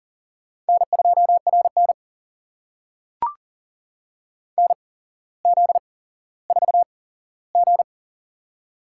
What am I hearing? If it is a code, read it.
Morse code, heard as a keyed tone, about 40 wpm, spaced as for 17 wpm, 700 Hz. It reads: N2RN NZ4G
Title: USA Call Signs - 4 Characters - 8x Character Spacing - 40wpm